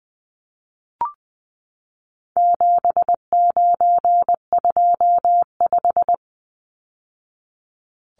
Morse code, heard as a keyed tone, about 20 wpm, 700 Hz. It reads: 7925
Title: Numbers - 4-Digit 20wpm